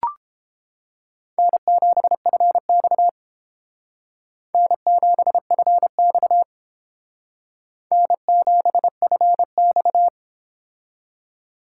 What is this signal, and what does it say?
Morse code, heard as a keyed tone, about 30 wpm, 700 Hz. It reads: N7FX N7FX N7FX